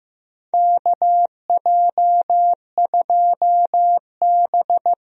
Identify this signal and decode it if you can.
Morse code, heard as a keyed tone, about 15 wpm, 700 Hz. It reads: KJ2B
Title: USA Call Signs - 4 Characters - 15wpm